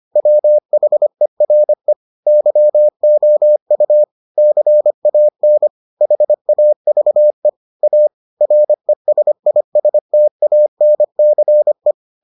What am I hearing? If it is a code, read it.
Morse code, heard as a keyed tone, about 25 wpm, 600 Hz. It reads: WHERE YOU CAN HAVE A RESISTANCE